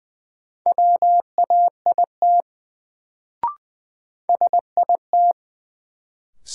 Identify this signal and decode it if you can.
Morse code, heard as a keyed tone, about 20 wpm, 700 Hz. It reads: WAIT SIT
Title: Sentences from Top 500 Words - New Word Drill - 20wpm